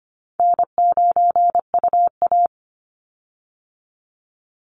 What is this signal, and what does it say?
Morse code, heard as a keyed tone, about 25 wpm, 700 Hz. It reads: N9UA
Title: Easy Call Signs Repeated 2x - 25wpm